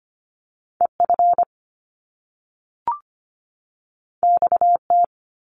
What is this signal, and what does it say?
Morse code, heard as a keyed tone, about 25 wpm, 700 Hz. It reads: EF XT
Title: Top Two Letter Combinations in Words 25wpm